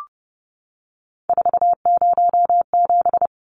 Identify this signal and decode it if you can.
Morse code, heard as a keyed tone, about 30 wpm, 700 Hz. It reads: 407
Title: Numbers - 3-Digit 30wpm